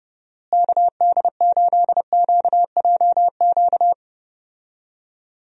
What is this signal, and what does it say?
Morse code, heard as a keyed tone, about 30 wpm, 700 Hz. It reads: KD8QJQ
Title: USA Call Signs - 6 Characters - Repeated 2x - 30wpm